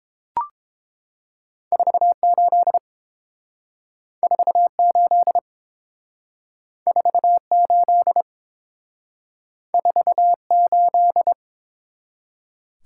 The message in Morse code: 48 48 48 48

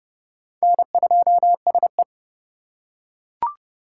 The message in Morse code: N2SE

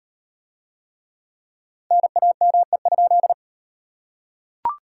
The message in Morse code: NAME?

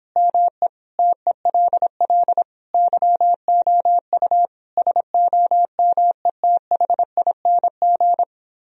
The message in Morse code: ME TELL YOU SOMETHING

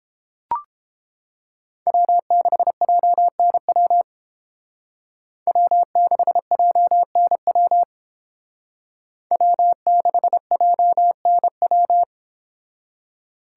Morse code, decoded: W6JNW W6JNW W6JNW

